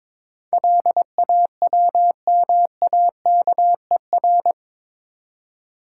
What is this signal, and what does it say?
Morse code, heard as a keyed tone, about 22 wpm, 700 Hz. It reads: LAWMAKER
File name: practice / 22wpm / Eight Letter Words - 22wpm